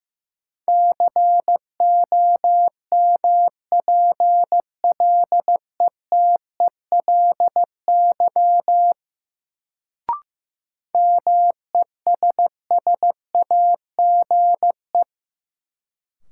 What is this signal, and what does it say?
Morse code, heard as a keyed tone, about 15 wpm, 700 Hz. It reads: COMPLETELY MESSAGE
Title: Sentences from Top 1100 Words - New Word Drill - 15wpm